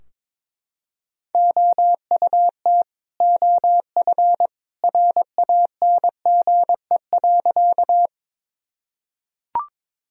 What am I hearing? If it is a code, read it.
Morse code, heard as a keyed tone, about 22 wpm, 700 Hz. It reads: OUT OF RANGE.
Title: Sentences from Top 1000 Words - Review Part 4 - 22wpm